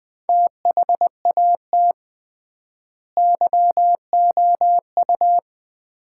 Morse code, heard as a keyed tone, about 20 wpm, 700 Hz. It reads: THAT YOU
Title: 3x Word-Spacing - Sets of 2 Words - 20wpm